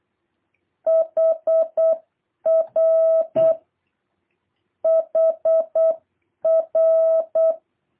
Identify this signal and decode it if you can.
Morse code, heard as a keyed tone, about 8 wpm, 660 Hz. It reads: HR HR
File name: jh8cbh / 2023cw